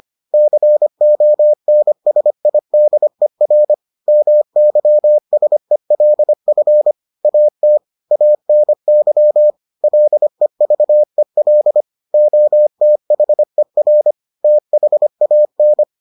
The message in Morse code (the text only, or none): CONSIDER MYSELF AT ANY LEVEL OTHER THAN